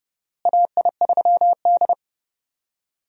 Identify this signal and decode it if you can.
Morse code, heard as a keyed tone, about 32 wpm, 700 Hz. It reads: AI3D
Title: USA Call Signs - 4 Characters - Repeated 3x - 30wpm